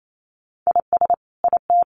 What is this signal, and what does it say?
Morse code, heard as a keyed tone, about 28 wpm, 700 Hz. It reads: IS IT
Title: Sentences from Top 1500 Words 28wpm